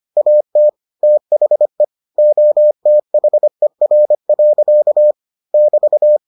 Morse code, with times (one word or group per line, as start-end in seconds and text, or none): 0.2-0.7 AT
1.0-1.8 THE
2.2-5.1 OTHER.
5.5-6.2 <BT>